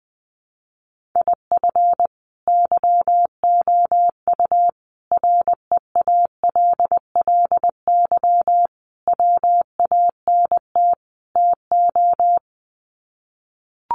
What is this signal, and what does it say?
Morse code, heard as a keyed tone, about 20 wpm, 700 Hz. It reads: IF YOU REALLY WANT TO
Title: Sets of 5 Words Encore2 20wpm